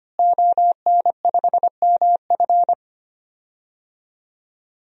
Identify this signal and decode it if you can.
Morse code, heard as a keyed tone, about 25 wpm, 700 Hz. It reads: ON5MF
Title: DXCC-100 Call Signs - 5 Characters - Repeated 3x - 25wpm